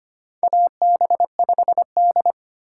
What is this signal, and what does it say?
Morse code, heard as a keyed tone, about 25 wpm, 700 Hz. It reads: AB5D
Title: USA Call Signs - 3x Character Spacing - 25wpm